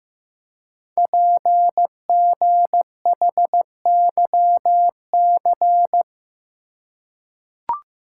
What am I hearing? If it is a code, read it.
Morse code, heard as a keyed tone, about 15 wpm, 700 Hz. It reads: PGHYC
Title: Code Groups - 5 Letters - 15wpm